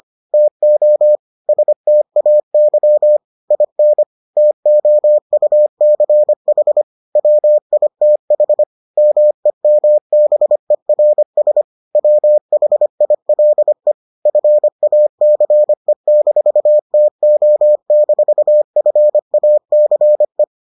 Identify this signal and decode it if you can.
Morse code, heard as a keyed tone, about 25 wpm, 600 Hz. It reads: TO STAY IN TOUCH WITH MEMBERS WHILE FACE-TO-FACE